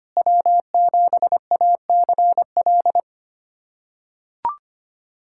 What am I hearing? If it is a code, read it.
Morse code, heard as a keyed tone, about 25 wpm, 700 Hz. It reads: W7ACL
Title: USA Callsign DE Callsign - 5 Characters - 25wpm